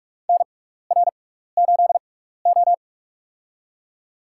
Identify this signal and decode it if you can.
Morse code, heard as a keyed tone, about 46 wpm, spaced as for 15 wpm, 700 Hz. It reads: NR8O
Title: USA Call Signs - 6x Character Spacing - 45wpm